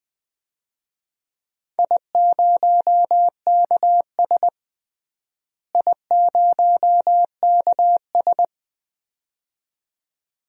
Morse code, heard as a keyed tone, about 20 wpm, 700 Hz. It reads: I0KS I0KS
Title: DXCC-100 Call Signs - 4 Characters - Repeated 2x - 20wpm